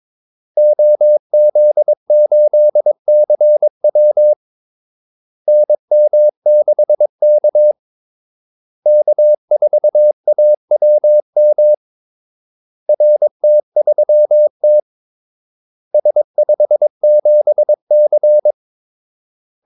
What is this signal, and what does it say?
Morse code, heard as a keyed tone, about 22 wpm, 600 Hz. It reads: OZ8CW NM6K K4AWM RT3T S57C
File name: Callsigns-22wpm